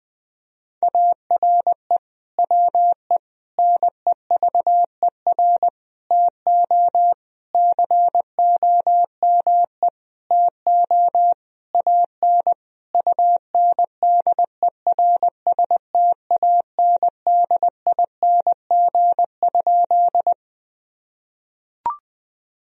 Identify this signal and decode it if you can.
Morse code, heard as a keyed tone, about 20 wpm, 700 Hz. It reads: ARE WE NEVER TO COME TO AN UNDERSTANDING?
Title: Sentences from Top 1000 Words - Review Part 3 - 20wpm